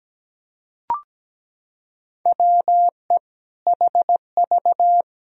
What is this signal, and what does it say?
Morse code, heard as a keyed tone, about 17 wpm, 700 Hz. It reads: WE HV
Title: Sentences from Top 100 Words and Abbreviations 17wpm